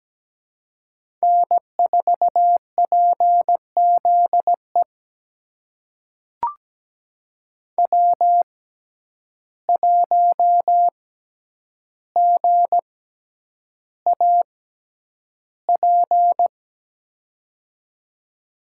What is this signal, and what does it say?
Morse code, heard as a keyed tone, about 17 wpm, spaced as for 9 wpm, 700 Hz. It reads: N4PZE W1GAP
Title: USA Call Signs - 5 Characters - 6x Character Spacing - 17wpm